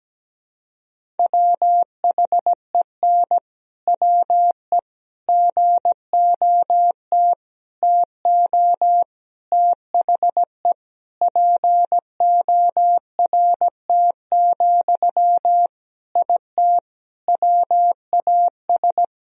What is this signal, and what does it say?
Morse code, heard as a keyed tone, about 17 wpm, 700 Hz. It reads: WHEN WE GOT TO THE PORT, IT WAS
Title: Sentences from Top 1000 Words - Review Part 1 - 17wpm